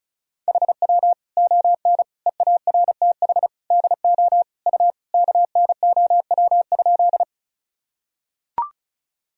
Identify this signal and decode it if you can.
Morse code, heard as a keyed tone, about 34 wpm, 700 Hz. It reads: HW ON EARTH DO U KNOW?
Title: Sentences from Top 200 Words and Abbreviations 35wpm